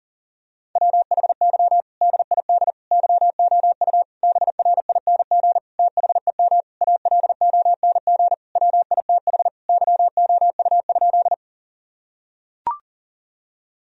Morse code, heard as a keyed tone, about 37 wpm, 700 Hz. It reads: WHY DID YOU BRING THEM ALONG WITH YOU?
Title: Sentences from Top 1000 Words - Review Part 3 - 40wpm